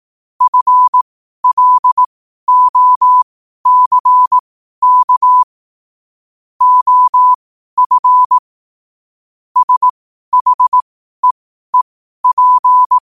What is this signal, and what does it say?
Morse code, heard as a keyed tone, about 18 wpm, spaced as for 12 wpm, 1000 Hz. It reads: FLOCK OF SHEEP
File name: qotdfiles / 12wpm